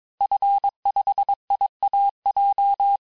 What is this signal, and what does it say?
Morse code, heard as a keyed tone, about 22 wpm, 800 Hz. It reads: F5IAJ
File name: cours-du-mercredi / audios / cours-854-liste-fin-QSO-vitesse-21-mpm